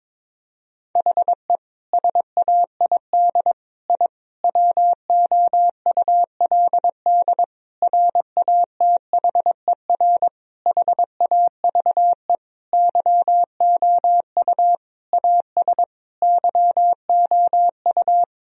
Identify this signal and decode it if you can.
Morse code, heard as a keyed tone, about 22 wpm, 700 Hz. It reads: HE SAID I WOULD RATHER HAVE YOU AS YOU
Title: Sentences from Top 1000 Words - Review Part 4 - 22wpm